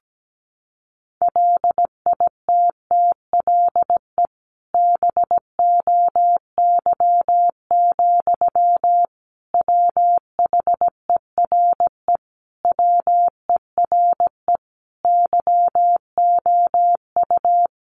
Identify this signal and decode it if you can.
Morse code, heard as a keyed tone, about 17 wpm, 700 Hz. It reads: LITTLE BOY, WHERE WERE YOU
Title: Sentences from Top 1000 Words 17wpm